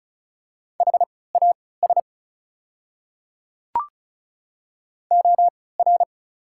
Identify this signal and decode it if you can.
Morse code, heard as a keyed tone, about 34 wpm, spaced as for 20 wpm, 700 Hz. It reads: HAS OR
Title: 3x Character Spacing - Top 300 Words - 35wpm